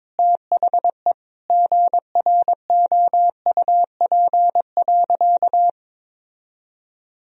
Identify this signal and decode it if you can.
Morse code, heard as a keyed tone, about 21 wpm, 700 Hz. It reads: THE GROUP.
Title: Sentences from Top 1000 Words - Review Part 1 - 22wpm